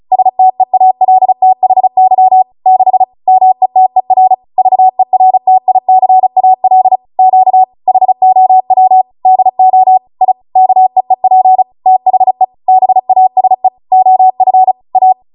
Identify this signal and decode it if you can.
Morse code, heard as a keyed tone, about 36 wpm, 750 Hz. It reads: STEALTHY 6 METER VERTICAL Q HOW DO I KEEP THE BASE OF A